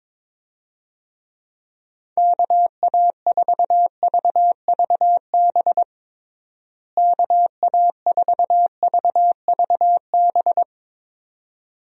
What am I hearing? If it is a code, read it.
Morse code, heard as a keyed tone, about 22 wpm, 700 Hz. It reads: KA4VVB KA4VVB
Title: USA Call Signs - 6 Characters - Repeated 2x - 22wpm